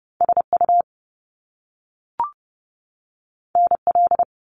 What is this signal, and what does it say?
Morse code, heard as a keyed tone, about 30 wpm, 700 Hz. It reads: SU NL
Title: Top Two Letter Combinations in Words 30wpm